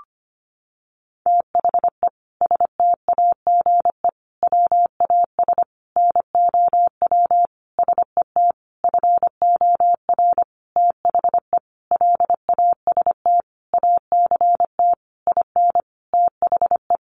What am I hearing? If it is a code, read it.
Morse code, heard as a keyed tone, about 25 wpm, 700 Hz. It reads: THE STAGE WAS NOW SET FOR THE LAST ACT IN THE
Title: Sentences from Top 1500 Words 25wpm